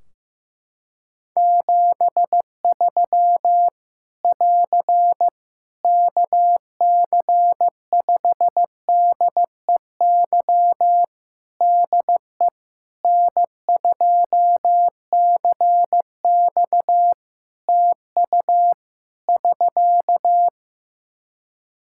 Morse code, heard as a keyed tone, about 15 wpm, 700 Hz. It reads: 73 <AR> KC5DEY DE N2CX TU <SK>
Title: Practice QSO - Exchange 4 - Ending - 15wpm